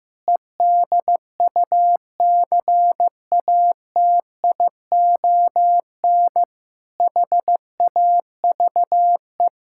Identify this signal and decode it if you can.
Morse code, heard as a keyed tone, about 15 wpm, 700 Hz. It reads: EDUCATION HAVE
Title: Sentences from Top 1000 Words - Review Part 6 - 15wpm